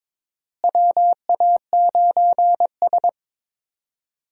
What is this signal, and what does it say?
Morse code, heard as a keyed tone, about 22 wpm, 700 Hz. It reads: WA9S
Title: USA Call Signs - 4 Characters - Repeated 3x - 22wpm